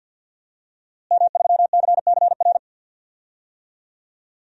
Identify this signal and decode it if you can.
Morse code, heard as a keyed tone, about 51 wpm, 700 Hz. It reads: M3CCR
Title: DXCC-100 Call Signs - 5 Characters - Repeated 2x - 50wpm